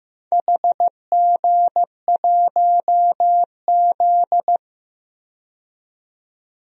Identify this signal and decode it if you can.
Morse code, heard as a keyed tone, about 15 wpm, 700 Hz. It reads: HG1Z
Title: DXCC-100 Call Signs - 4 Characters - 15wpm